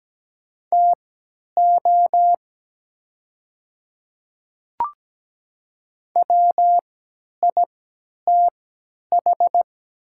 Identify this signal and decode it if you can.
Morse code, heard as a keyed tone, about 17 wpm, spaced as for 10 wpm, 700 Hz. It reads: TO WITH
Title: 3x Character Spacing - Top 100 Words - 17wpm